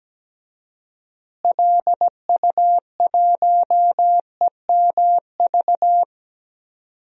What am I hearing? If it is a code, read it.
Morse code, heard as a keyed tone, about 17 wpm, 700 Hz. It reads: LU1EMV
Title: DXCC-100 Call Signs - 6 Characters - Repeated 2x - 17wpm